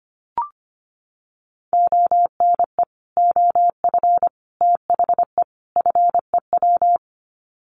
Morse code, heard as a keyed tone, about 25 wpm, 700 Hz. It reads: ONE OF THE FEW
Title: Sets of 4 Words 25wpm